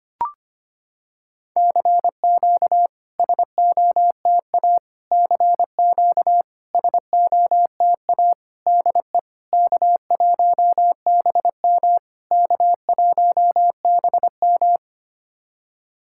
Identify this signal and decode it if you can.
Morse code, heard as a keyed tone, about 25 wpm, 700 Hz. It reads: CQ SOTA CQ SOTA DE K1BM K1BM
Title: SOTA QSO Practice 25wpm